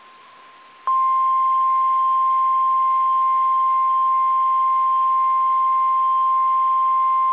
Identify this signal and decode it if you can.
Emergency Alert System (SAME) tones (the 1050 Hz attention tone). It is heard from 0.9 to 6.6 s.
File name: eas / KSDZKDJL / KSDZKDJL